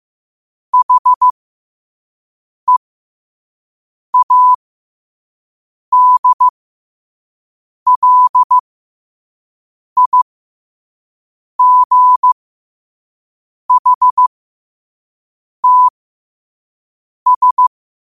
Morse code, heard as a keyed tone, about 15 wpm, spaced as for 5 wpm, 1000 Hz. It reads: HEADLIGHTS
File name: qotdfiles / 5wpm